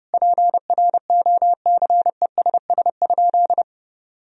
Morse code, heard as a keyed tone, about 30 wpm, 700 Hz. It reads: PROCESS?
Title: Sentences from Top 1000 Words - Review Part 3 - 30wpm